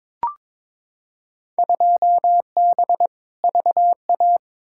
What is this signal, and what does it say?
Morse code, heard as a keyed tone, about 21 wpm, 700 Hz. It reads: 2B VA
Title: Field Day Exchanges Encore - 22wpm